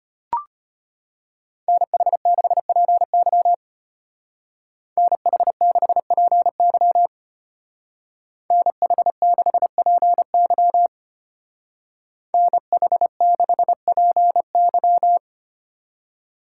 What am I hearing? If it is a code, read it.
Morse code, heard as a keyed tone, about 30 wpm, 700 Hz. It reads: NH6PY NH6PY NH6PY NH6PY
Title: Speed Racing - USA 2x2 Call Signs 25wpm